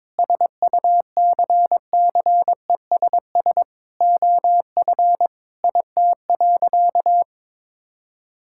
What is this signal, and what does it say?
Morse code, heard as a keyed tone, about 22 wpm, 700 Hz. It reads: SUCCESS OF IT.
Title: Sentences from Top 1000 Words - Review Part 6 - 22wpm